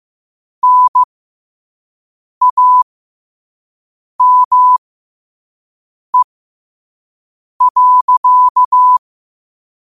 Morse code, heard as a keyed tone, about 15 wpm, spaced as for 5 wpm, 1000 Hz. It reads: NAME.
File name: qotdfiles / 5wpm